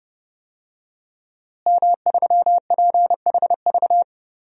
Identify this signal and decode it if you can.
Morse code, heard as a keyed tone, about 30 wpm, 700 Hz. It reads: M3PHV
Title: DXCC-100 Call Signs - 5 Characters - Repeated 2x - 30wpm